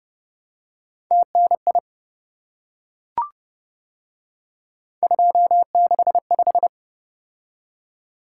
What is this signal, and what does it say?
Morse code, heard as a keyed tone, about 29 wpm, 700 Hz. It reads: TNI 265